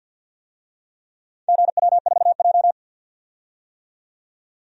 Morse code, heard as a keyed tone, about 46 wpm, 700 Hz. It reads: GW4J
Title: DXCC-100 Call Signs - 4 Characters - 50wpm